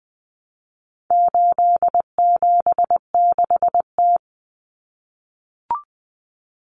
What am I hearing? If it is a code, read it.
Morse code, heard as a keyed tone, about 20 wpm, 700 Hz. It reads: 876T
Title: Common Cut Numbers - 4-Digit 20wpm